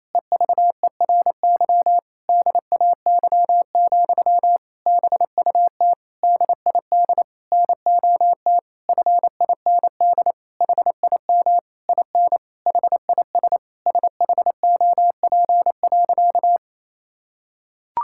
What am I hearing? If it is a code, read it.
Morse code, heard as a keyed tone, about 28 wpm, 700 Hz. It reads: EVERY DAY, BUT DID NOT FIND HIM IN HIS SHOP.
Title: Sentences from Top 1600 Words 28wpm